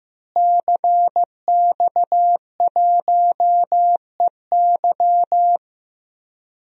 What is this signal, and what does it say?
Morse code, heard as a keyed tone, about 16 wpm, 700 Hz. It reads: CX1EY
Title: DXCC-100 Call Signs - 5 Characters - Repeated 3x - 15wpm